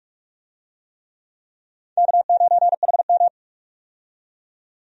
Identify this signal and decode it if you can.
Morse code, heard as a keyed tone, about 44 wpm, 700 Hz. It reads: K9HM
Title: USA Call Signs - 4 Characters - 45wpm